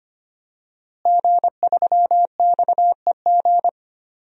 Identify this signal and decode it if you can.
Morse code, heard as a keyed tone, about 25 wpm, 700 Hz. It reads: G3XEG
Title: DXCC-100 Call Signs - 5 Characters - Repeated 3x - 25wpm